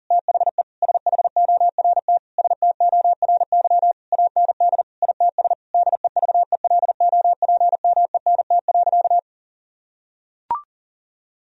Morse code, heard as a keyed tone, about 37 wpm, 700 Hz. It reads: THE SHORT STORY AND ITS DEVELOPMENT.
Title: Sentences from Top 1000 Words - Review Part 4 - 40wpm